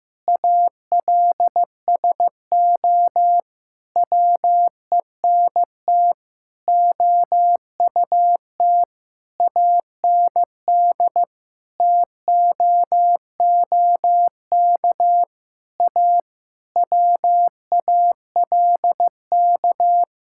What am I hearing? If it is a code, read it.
Morse code, heard as a keyed tone, about 16 wpm, 700 Hz. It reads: ALSO WENT OUT AND TOOK A WALK
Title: Sentences from Top 1600 Words 15wpm